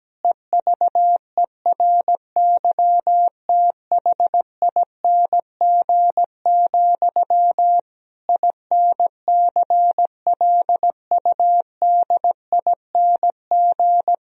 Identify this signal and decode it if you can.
Morse code, heard as a keyed tone, about 17 wpm, 700 Hz. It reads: EVERYTHING, INCLUDING
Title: Sentences from Top 1000 Words - Review Part 4 - 17wpm